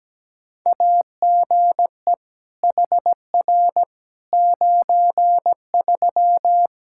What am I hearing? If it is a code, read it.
Morse code, heard as a keyed tone, about 17 wpm, 700 Hz. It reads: AGE HR 93
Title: Practice QSO - Exchange 3 - More Info - 17wpm